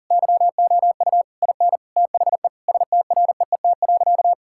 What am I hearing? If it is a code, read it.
Morse code, heard as a keyed tone, about 37 wpm, 700 Hz. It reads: YOU IN THE STREET.
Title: Sentences from Top 1000 Words - Review Part 6 - 40wpm